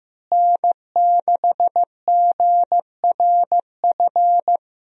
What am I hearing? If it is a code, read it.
Morse code, heard as a keyed tone, about 15 wpm, 700 Hz. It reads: N6GRF
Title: USA Call Signs - 5 Characters - Repeated 3x - 15wpm